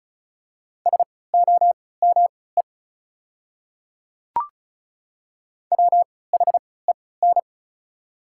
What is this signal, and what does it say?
Morse code, heard as a keyed tone, about 34 wpm, spaced as for 20 wpm, 700 Hz. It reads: SOME WHEN
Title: 3x Character Spacing - Top 200 Words - 35wpm